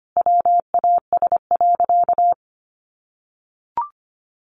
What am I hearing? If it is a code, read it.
Morse code, heard as a keyed tone, about 25 wpm, 700 Hz. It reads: WAS.
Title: Sentences from Top 300 Words 25wpm